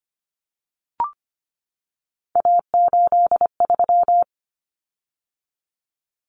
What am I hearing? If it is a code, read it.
Morse code, heard as a keyed tone, about 25 wpm, 700 Hz. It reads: A83
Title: Common Cut Numbers - 3-Digit 25wpm